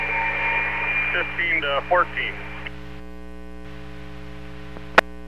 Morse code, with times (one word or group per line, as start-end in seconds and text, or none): none